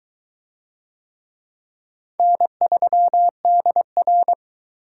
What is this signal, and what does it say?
Morse code, heard as a keyed tone, about 22 wpm, 700 Hz. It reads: N3DR